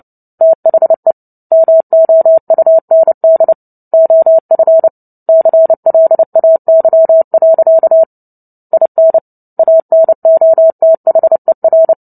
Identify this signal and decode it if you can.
Morse code, heard as a keyed tone, about 30 wpm, 650 Hz. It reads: THE MOUND OF CLAY. IN ANOTHER
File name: Book Two / 03-Chapter 3 - 29 WPM